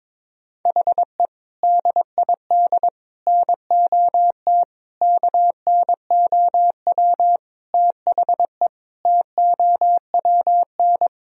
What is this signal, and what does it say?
Morse code, heard as a keyed tone, about 22 wpm, 700 Hz. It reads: HE DID NOT KNOW THE TOWN
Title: Sentences from Top 1000 Words - Review Part 2 - 22wpm